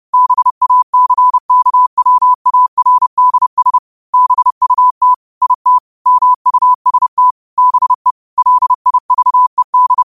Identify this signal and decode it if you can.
Morse code, heard as a keyed tone, about 30 wpm, 1000 Hz. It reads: BACKWARDS BUT IT MUST BE LIVED
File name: qotdfiles / 30wpm